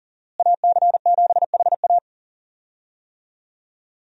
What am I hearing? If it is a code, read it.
Morse code, heard as a keyed tone, about 40 wpm, 700 Hz. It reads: AC7HA
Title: USA Call Signs - 5 Characters - Repeated 2x - 40wpm